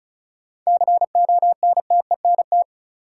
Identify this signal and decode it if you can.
Morse code, heard as a keyed tone, about 36 wpm, 700 Hz. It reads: CONTENT
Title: Sentences from Top 900 Words - New Word Drill - 35wpm